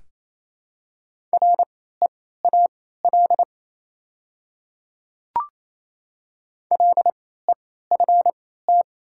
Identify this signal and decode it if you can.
Morse code, heard as a keyed tone, about 27 wpm, spaced as for 16 wpm, 700 Hz. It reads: REAL LEFT
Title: 3x Character Spacing - Top 300 Words - 28wpm